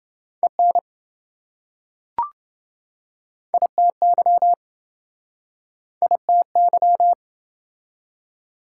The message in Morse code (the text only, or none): EN ITY ITY